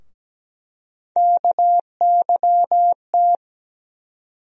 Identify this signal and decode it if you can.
Morse code, heard as a keyed tone, about 17 wpm, 700 Hz. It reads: KYT
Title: USA Call Sign Postfixes - 3 Letters 17wpm